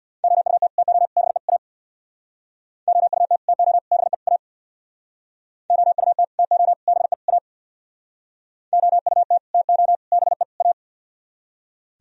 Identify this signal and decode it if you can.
Morse code, heard as a keyed tone, about 64 wpm, 700 Hz. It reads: OUT TO BE A OUT TO BE A OUT TO BE A OUT TO BE A